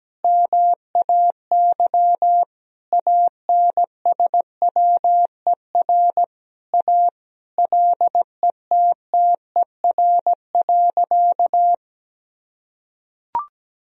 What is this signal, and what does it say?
Morse code, heard as a keyed tone, about 17 wpm, 700 Hz. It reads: MAY ANSWER A LETTER.
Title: Sentences from Top 1000 Words - Review Part 1 - 17wpm